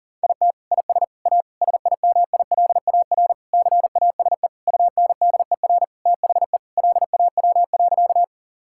Morse code, heard as a keyed tone, about 40 wpm, 700 Hz. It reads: IT IS A SIMILAR CASE UNDER THE LAW.
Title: Sentences from Top 1000 Words - Review Part 5 - 40wpm